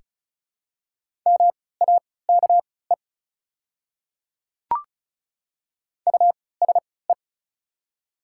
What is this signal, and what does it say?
Morse code, heard as a keyed tone, about 36 wpm, spaced as for 20 wpm, 690 Hz. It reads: MAKE USE
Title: 3x Character Spacing - Top 200 Words - 35wpm